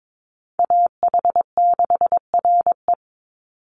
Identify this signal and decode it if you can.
Morse code, heard as a keyed tone, about 22 wpm, 700 Hz. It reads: AH6RE
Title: Call Signs Repeated 3x - 22wpm